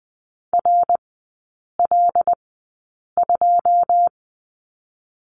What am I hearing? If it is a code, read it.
Morse code, heard as a keyed tone, about 20 wpm, 700 Hz. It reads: R L 2
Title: 107 Lesson 18e - Letter m - Three Characters 20wpm